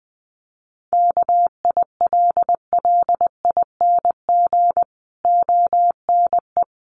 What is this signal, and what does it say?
Morse code, heard as a keyed tone, about 19 wpm, 700 Hz. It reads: KILLING ONE
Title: A Sound Of Thunder 20wpm